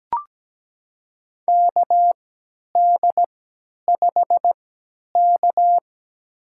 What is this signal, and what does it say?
Morse code, heard as a keyed tone, about 17 wpm, spaced as for 10 wpm, 700 Hz. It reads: KD5K
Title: USA Call Signs - 4 Characters - 3x Character Spacing - 17wpm